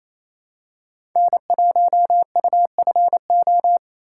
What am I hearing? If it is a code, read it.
Morse code, heard as a keyed tone, about 28 wpm, 700 Hz. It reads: N1UFO
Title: USA Call Signs - 3x Character Spacing - 28wpm